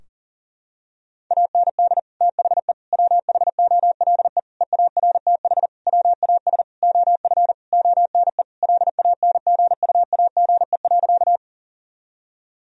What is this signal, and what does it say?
Morse code, heard as a keyed tone, about 39 wpm, 700 Hz. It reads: AND THE WHOLE EARTH WAS OF ONE LANGUAGE.
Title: Sentences from Top 1000 Words - Review Part 3 - 40wpm